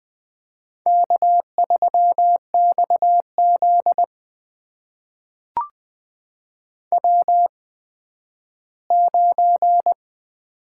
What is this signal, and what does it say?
Morse code, heard as a keyed tone, about 20 wpm, spaced as for 12 wpm, 700 Hz. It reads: K3XZ W9
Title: USA Call Signs - 4 Characters - 8x Character Spacing - 20wpm